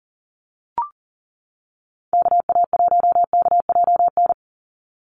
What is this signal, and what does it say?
Morse code, heard as a keyed tone, about 40 wpm, 700 Hz. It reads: KA1KJN
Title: USA Call Signs 40wpm